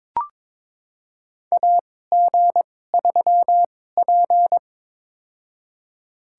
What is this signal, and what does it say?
Morse code, heard as a keyed tone, about 22 wpm, spaced as for 16 wpm, 700 Hz. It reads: AG3P
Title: USA Call Signs - 2x Character Spacing - 22wpm